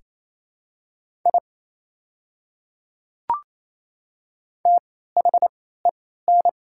Morse code, heard as a keyed tone, about 28 wpm, spaced as for 16 wpm, 690 Hz. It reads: I THEN
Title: 3x Character Spacing - Top 100 Words - 28wpm